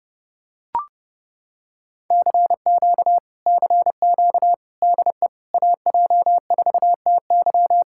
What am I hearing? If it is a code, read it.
Morse code, heard as a keyed tone, about 29 wpm, 700 Hz. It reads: CQ CQ DE AJ4TY